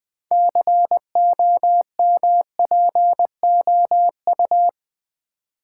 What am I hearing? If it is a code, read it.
Morse code, heard as a keyed tone, about 20 wpm, 700 Hz. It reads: COMPOU
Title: Build A Word - 3 Letters At A Time - 8-Letter Words - 20wpm